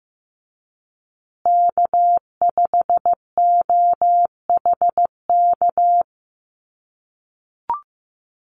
Morse code, heard as a keyed tone, about 15 wpm, 700 Hz. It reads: K5OHK
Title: USA Call Signs Repeated 2x 15wpm